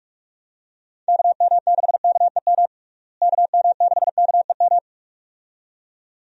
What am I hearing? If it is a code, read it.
Morse code, heard as a keyed tone, about 42 wpm, 700 Hz. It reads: KM6KEM KM6KEM
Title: USA Call Signs - 6 Characters - Repeated 2x - 45wpm